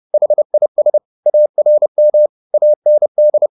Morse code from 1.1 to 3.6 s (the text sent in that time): ARM AND